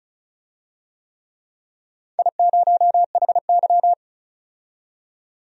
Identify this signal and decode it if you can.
Morse code, heard as a keyed tone, about 36 wpm, 700 Hz. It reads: I0HY
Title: DXCC-100 Call Signs - 4 Characters - 35wpm